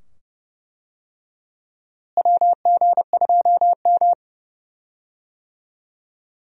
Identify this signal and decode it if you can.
Morse code, heard as a keyed tone, about 31 wpm, 700 Hz. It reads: WG2M
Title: USA Call Signs - 4 Characters - Repeated 2x - 30wpm